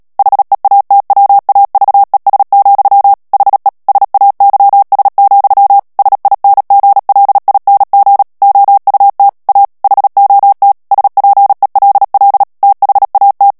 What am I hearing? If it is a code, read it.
Morse code, heard as a keyed tone, about 37 wpm, 800 Hz. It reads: HEATWAVES, HE SAYS, SINGLING OUT A HOT SPELL THAT
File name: morse mp3 / ANGLAIS texte clair 2 vitesse 35 mots minute